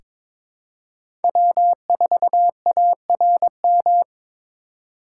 Morse code, heard as a keyed tone, about 21 wpm, 700 Hz. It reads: W4ARM
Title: USA Call Signs - 5 Characters - 6x Character Spacing - 22wpm